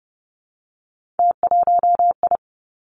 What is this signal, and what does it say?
Morse code, heard as a keyed tone, about 30 wpm, 700 Hz. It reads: T1I
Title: Three Challenging Letter-Number Combinations 30wpm